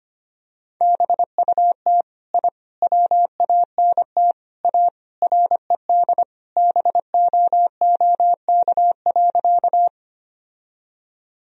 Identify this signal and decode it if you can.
Morse code, heard as a keyed tone, about 25 wpm, 700 Hz. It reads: BUT I WANT A RED BOOK.
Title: Sentences from Top 1000 Words - Review Part 2 - 25wpm